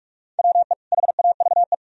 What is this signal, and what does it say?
Morse code, heard as a keyed tone, about 56 wpm, 690 Hz. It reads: WE HAVE